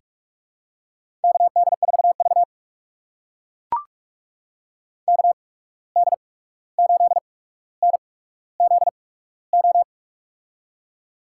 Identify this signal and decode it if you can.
Morse code, heard as a keyed tone, about 44 wpm, spaced as for 17 wpm, 700 Hz. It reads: KD4V KD8NZO
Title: USA Call Signs - 8x Character Spacing - 45wpm